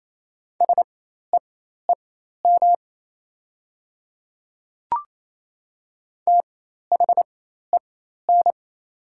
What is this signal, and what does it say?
Morse code, heard as a keyed tone, about 28 wpm, spaced as for 13 wpm, 690 Hz. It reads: SEEM THEN
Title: 4x Character Spacing - Top 500 Words - 28wpm